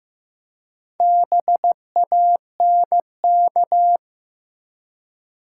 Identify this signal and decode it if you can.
Morse code, heard as a keyed tone, about 15 wpm, 700 Hz. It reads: BANK
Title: Four Letter Words - 15wpm